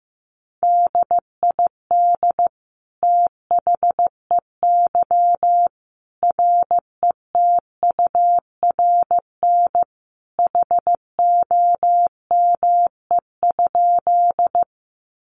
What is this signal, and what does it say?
Morse code, heard as a keyed tone, about 15 wpm, 700 Hz. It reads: DID THEY RETURN HOME?
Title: Sentences from Top 800 Words 15wpm